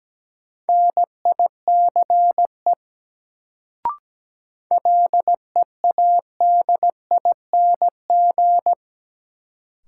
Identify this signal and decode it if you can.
Morse code, heard as a keyed tone, about 17 wpm, 700 Hz. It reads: NICE LEADING
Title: Sentences from Top 1100 Words - New Word Drill Alt - 17wpm